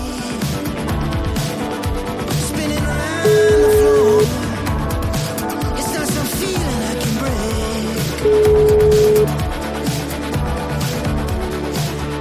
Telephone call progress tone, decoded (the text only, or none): ringback tone